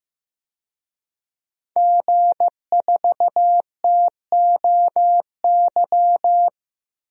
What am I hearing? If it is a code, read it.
Morse code, heard as a keyed tone, about 15 wpm, 700 Hz. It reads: G4TOY
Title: DXCC-100 Call Signs - 5 Characters - 15wpm